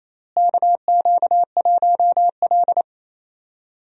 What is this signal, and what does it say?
Morse code, heard as a keyed tone, about 28 wpm, 700 Hz. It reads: KQ1L
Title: USA Call Signs - 4 Characters - Repeated 3x - 28wpm